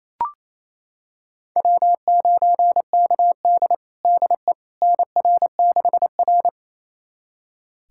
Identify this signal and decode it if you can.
Morse code, heard as a keyed tone, about 28 wpm, 700 Hz. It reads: W9KD DE NR6R
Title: USA Callsign DE Callsign - 4 Characters - 28wpm